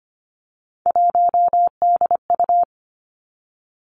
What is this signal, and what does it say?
Morse code, heard as a keyed tone, about 25 wpm, 700 Hz. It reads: 1DU
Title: Three Challenging Letter-Number Combinations 25wpm